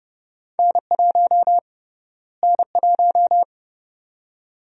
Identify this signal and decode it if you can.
Morse code, heard as a keyed tone, about 31 wpm, 700 Hz. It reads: N1 N1
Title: USA Call Sign Prefixes - Repeated 2x - 30wpm